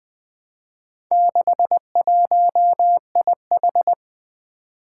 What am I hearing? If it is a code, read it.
Morse code, heard as a keyed tone, about 20 wpm, 700 Hz. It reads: 61IH
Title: Four Challenging Letter-Number Combinations 20wpm